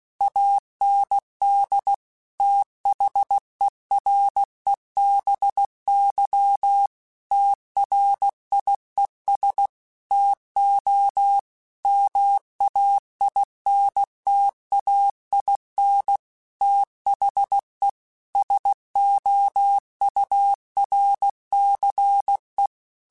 Morse code, heard as a keyed tone, about 16 wpm, 800 Hz. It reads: AND THEREBY TRIES TO MAINTAIN THE SOURCE